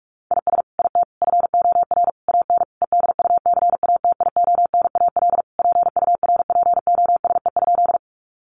Morse code, heard as a keyed tone, about 44 wpm, 700 Hz. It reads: IS IT FOR AN EDUCATIONAL PURPOSE?
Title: Sentences from Top 1200 Words 45wpm